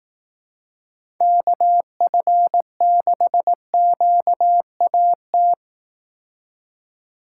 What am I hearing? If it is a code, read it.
Morse code, heard as a keyed tone, about 18 wpm, 700 Hz. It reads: KF6QAT
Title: Speed Racing - USA 2x3 Call Signs 15wpm